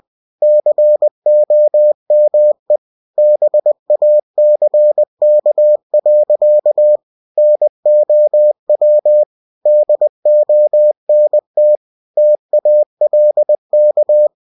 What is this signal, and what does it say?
Morse code, heard as a keyed tone, about 20 wpm, 600 Hz. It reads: COME BACK. NOW DONT TALK